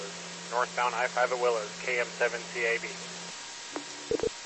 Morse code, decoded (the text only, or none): none